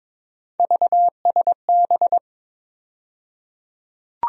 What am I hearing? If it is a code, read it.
Morse code, heard as a keyed tone, about 21 wpm, 700 Hz. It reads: VSB
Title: USA Call Sign Postfixes - 3 Letters 22wpm